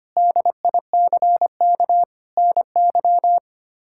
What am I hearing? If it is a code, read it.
Morse code, heard as a keyed tone, about 25 wpm, 700 Hz. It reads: DICK NY
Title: K1USN SST QSO Practice - 25wpm